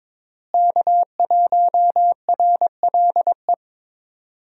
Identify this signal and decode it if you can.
Morse code, heard as a keyed tone, about 22 wpm, 700 Hz. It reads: K1RLE
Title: USA Call Signs - 5 Characters - Repeated 3x - 22wpm